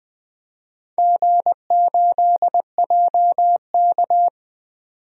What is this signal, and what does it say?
Morse code, heard as a keyed tone, about 20 wpm, 700 Hz. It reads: G8JK
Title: DXCC-100 Call Signs - 4 Characters - 20wpm